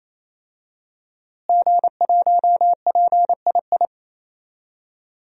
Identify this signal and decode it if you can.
Morse code, heard as a keyed tone, about 27 wpm, 700 Hz. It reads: G1PII
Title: DXCC-100 Call Signs - 5 Characters - Repeated 3x - 28wpm